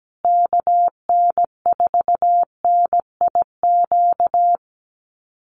Morse code, heard as a keyed tone, about 17 wpm, 700 Hz. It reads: KN4NIQ
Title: USA Call Signs Repeated 3x 17wpm